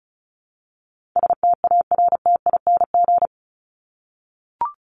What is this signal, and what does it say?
Morse code, heard as a keyed tone, about 34 wpm, 700 Hz. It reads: STARTING
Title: ING Suffix - Top 100 Words with Base Words 35wpm